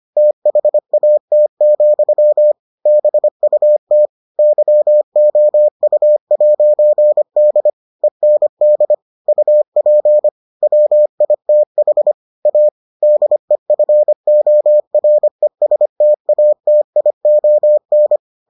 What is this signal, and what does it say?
Morse code, heard as a keyed tone, about 25 wpm, 600 Hz. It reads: THAT, BUT YOU'D END UP WITH A DEFORESTATION